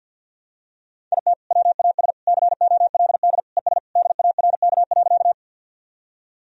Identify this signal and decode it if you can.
Morse code, heard as a keyed tone, about 51 wpm, 700 Hz. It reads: IT WAS COLD ES DARK.
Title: Sentences from Top 500 Words and Abbreviations 50wpm